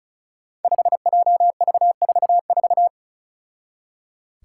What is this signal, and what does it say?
Morse code, heard as a keyed tone, about 36 wpm, 700 Hz. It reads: 5JV44
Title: Five Challenging Letter-Number Combinations 35wpm